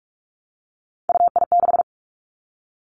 Morse code, heard as a keyed tone, about 46 wpm, 700 Hz. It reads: UI6
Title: Three Challenging Letter-Number Combinations 45wpm